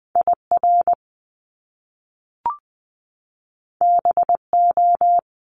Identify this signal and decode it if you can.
Morse code, heard as a keyed tone, about 20 wpm, 700 Hz. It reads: IR BO
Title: Top Two Letter Combinations in Words 20wpm